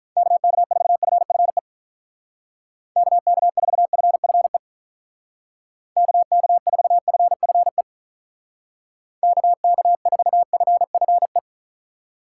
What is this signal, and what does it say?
Morse code, heard as a keyed tone, about 46 wpm, 700 Hz. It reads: KK4FFE KK4FFE KK4FFE KK4FFE